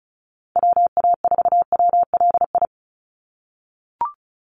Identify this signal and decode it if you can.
Morse code, heard as a keyed tone, about 34 wpm, 700 Hz. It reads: WA4WLI
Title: Call Signs - 35wpm